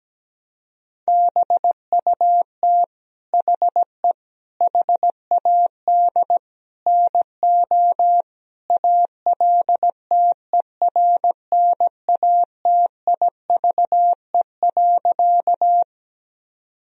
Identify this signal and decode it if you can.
Morse code, heard as a keyed tone, about 17 wpm, 700 Hz. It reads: BUT HE HAD NO ALTERNATIVE.
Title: Sentences from Top 1000 Words - Review Part 4 - 17wpm